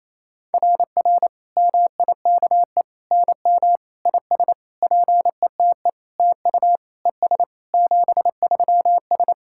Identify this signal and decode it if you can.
Morse code, heard as a keyed tone, about 28 wpm, 690 Hz. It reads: RR MIKE NM IS PETE TU ES 73S
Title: SOTA QSO Practice 28wpm